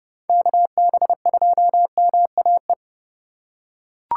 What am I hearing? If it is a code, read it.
Morse code, heard as a keyed tone, about 28 wpm, 700 Hz. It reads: KB2MAE